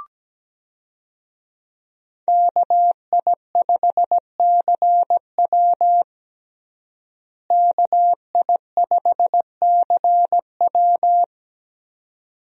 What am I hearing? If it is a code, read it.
Morse code, heard as a keyed tone, about 17 wpm, 700 Hz. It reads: KI5CW KI5CW
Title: USA Call Signs - 5 Characters - Repeated 2x - 17wpm